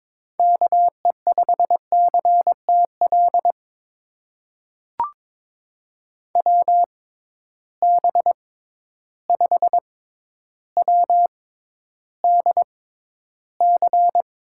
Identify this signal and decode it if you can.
Morse code, heard as a keyed tone, about 22 wpm, spaced as for 11 wpm, 700 Hz. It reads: KE5CTL WB5WDC
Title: USA Call Signs - 6 Characters - 6x Character Spacing - 22wpm